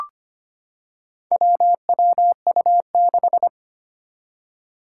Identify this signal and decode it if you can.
Morse code, heard as a keyed tone, about 25 wpm, 700 Hz. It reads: WWU6
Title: Four Challenging Letter-Number Combinations 25wpm